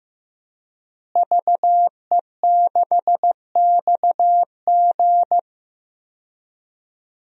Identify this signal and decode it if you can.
Morse code, heard as a keyed tone, about 15 wpm, 700 Hz. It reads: VE6XG
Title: DXCC-100 Call Signs - 5 Characters - Repeated 2x - 15wpm